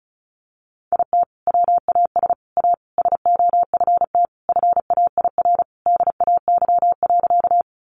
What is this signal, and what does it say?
Morse code, heard as a keyed tone, about 34 wpm, 700 Hz. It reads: IT WAS A SOFT FAIR DAY.
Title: Sentences from Top 1500 Words 35wpm